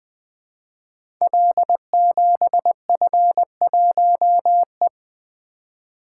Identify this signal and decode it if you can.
Morse code, heard as a keyed tone, about 20 wpm, 700 Hz. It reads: L7F1E
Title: Code Groups - 5 Letter:Numbers - 20wpm